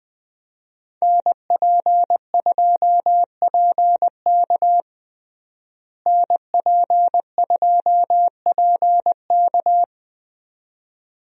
Morse code, heard as a keyed tone, about 20 wpm, 700 Hz. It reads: NP2PK NP2PK
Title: USA Call Signs - 5 Characters - Repeated 2x - 20wpm